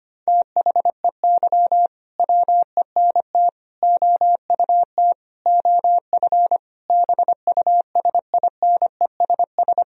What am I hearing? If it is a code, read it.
Morse code, heard as a keyed tone, about 25 wpm, 700 Hz. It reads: THEY WENT OUT OF BUSINESS
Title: Sentences from Top 1000 Words - Review Part 4 - 25wpm